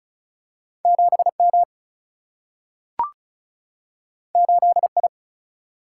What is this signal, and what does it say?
Morse code, heard as a keyed tone, about 36 wpm, 700 Hz. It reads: 7 M 8 I
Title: Single-Letter Single-Number 35wpm